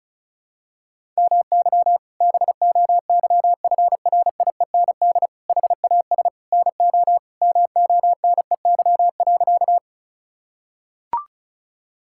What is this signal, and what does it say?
Morse code, heard as a keyed tone, about 36 wpm, 700 Hz. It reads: MY BOYFRIEND HAS NO MONEY.
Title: Sentences from Top 1000 Words - Review Part 5 - 35wpm